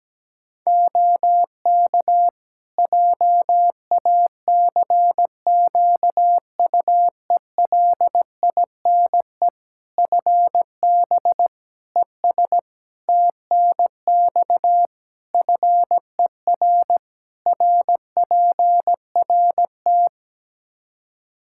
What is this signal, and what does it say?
Morse code, heard as a keyed tone, about 17 wpm, 700 Hz. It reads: OK JACQUELINE FB ES TNX FER RPRT
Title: K1USN SST QSO Practice 17wpm